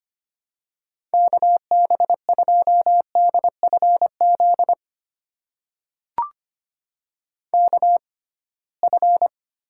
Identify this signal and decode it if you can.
Morse code, heard as a keyed tone, about 25 wpm, spaced as for 19 wpm, 700 Hz. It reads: KB2DFZ KF
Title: USA Call Signs - 6x Character Spacing - 25wpm